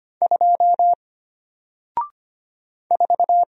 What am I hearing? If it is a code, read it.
Morse code, heard as a keyed tone, about 25 wpm, 700 Hz. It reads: 2 4
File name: Single-Letter-Number-25wpm